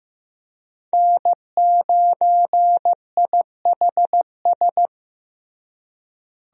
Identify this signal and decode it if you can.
Morse code, heard as a keyed tone, about 15 wpm, 700 Hz. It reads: N9IHS
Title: USA Call Signs - 5 Characters - Repeated 3x - 15wpm